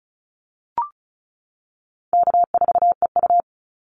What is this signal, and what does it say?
Morse code, heard as a keyed tone, about 34 wpm, 700 Hz. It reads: K4EU
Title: Call Signs - 35wpm